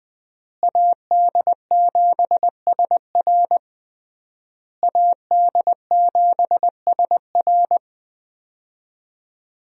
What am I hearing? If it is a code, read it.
Morse code, heard as a keyed tone, about 20 wpm, 700 Hz. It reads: AD7SR AD7SR
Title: USA Call Signs - 5 Characters - Repeated 3x - 20wpm